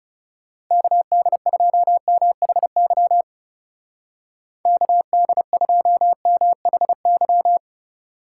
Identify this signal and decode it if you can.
Morse code, heard as a keyed tone, about 36 wpm, 700 Hz. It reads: KD2MHY KD2MHY
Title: Speed Racing - USA 2x3 Call Signs 30wpm